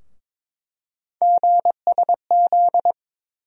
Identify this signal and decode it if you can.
Morse code, heard as a keyed tone, about 22 wpm, 700 Hz. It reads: GSZ
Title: USA Call Sign Postfixes - 3 Letters 22wpm